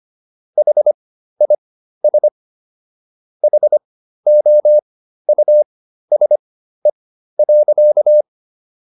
Morse code, heard as a keyed tone, about 25 wpm, spaced as for 13 wpm, 600 Hz. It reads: HIS HOUSE.